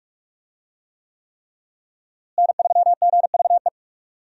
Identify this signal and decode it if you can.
Morse code, heard as a keyed tone, about 46 wpm, 700 Hz. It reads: N3GVE
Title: USA Call Signs - 5 Characters - Repeated 3x - 45wpm